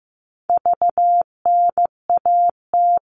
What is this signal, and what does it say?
Morse code, heard as a keyed tone, about 15 wpm, 700 Hz. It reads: VNAT
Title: All Cut Numbers - 4-Digit 15wpm